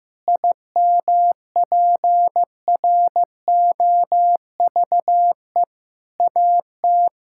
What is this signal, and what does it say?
Morse code, heard as a keyed tone, about 15 wpm, 700 Hz. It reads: IMPROVE AT
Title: Sentences from Top 1000 Words - Review Part 5 - 15wpm